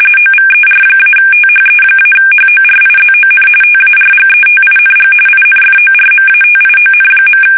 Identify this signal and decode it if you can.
RTTY, about 45.45 baud. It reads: OVER-THE-HORIZON TARGETING INFORMATION.
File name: fsk850 / battleshipnj